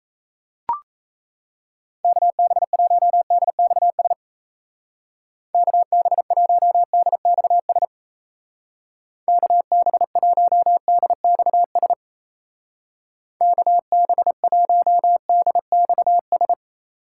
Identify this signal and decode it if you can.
Morse code, heard as a keyed tone, about 36 wpm, 700 Hz. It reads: KB1DXS KB1DXS KB1DXS KB1DXS